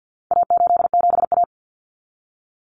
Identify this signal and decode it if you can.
Morse code, heard as a keyed tone, about 48 wpm, 700 Hz. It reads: A87A
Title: Common Cut Numbers - 4-Digit 50wpm